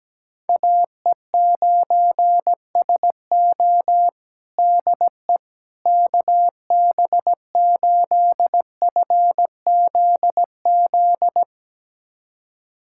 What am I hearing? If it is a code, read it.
Morse code, heard as a keyed tone, about 17 wpm, 700 Hz. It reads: AE9SO DE KB8FZZ
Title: USA Callsign DE Callsign 17wpm